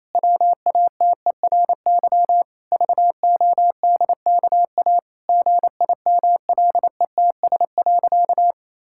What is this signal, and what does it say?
Morse code, heard as a keyed tone, about 28 wpm, 700 Hz. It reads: WATERY VODKA GIMLETS.
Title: Pangram Challenge v2 - 28wpm